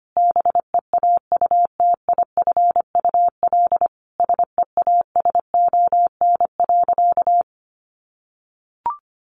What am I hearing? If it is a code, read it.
Morse code, heard as a keyed tone, about 25 wpm, 700 Hz. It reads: BEAUTIFUL SEASON.
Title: Sentences from Top 1400 Words 25wpm